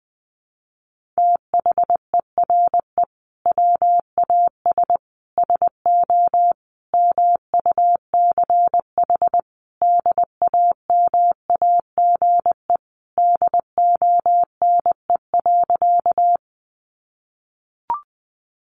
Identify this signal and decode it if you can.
Morse code, heard as a keyed tone, about 20 wpm, 700 Hz. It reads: THERE WAS SO MUCH DAMAGE DONE.
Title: Sentences from Top 1100 Words 20wpm